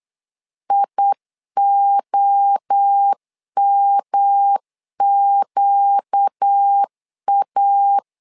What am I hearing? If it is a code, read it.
Morse code, heard as a keyed tone, about 8 wpm, 800 Hz. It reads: IOMQA
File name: 4 Q 8wpm